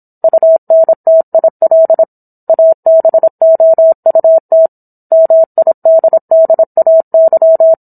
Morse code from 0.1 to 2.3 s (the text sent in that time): UNTIL